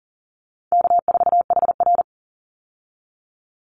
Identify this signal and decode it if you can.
Morse code, heard as a keyed tone, about 40 wpm, 700 Hz. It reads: K4HR
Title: Call Signs Repeated 3x - 40wpm